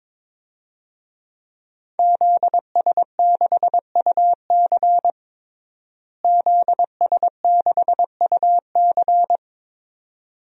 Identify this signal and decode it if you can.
Morse code, heard as a keyed tone, about 22 wpm, 700 Hz. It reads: ZS6UC ZS6UC
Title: DXCC-100 Call Signs - 5 Characters - Repeated 2x - 22wpm